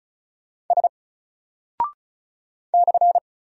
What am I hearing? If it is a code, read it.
Morse code, heard as a keyed tone, about 36 wpm, 700 Hz. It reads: S /
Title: Single Character (Letter:Number:Punctuation:Prosign) 35wpm